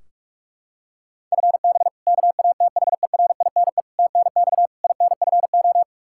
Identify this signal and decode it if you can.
Morse code, heard as a keyed tone, about 41 wpm, 700 Hz. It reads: FB KATHERINE TNX INFO
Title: Practice QSO - Exchange 3 - More Info - 45wpm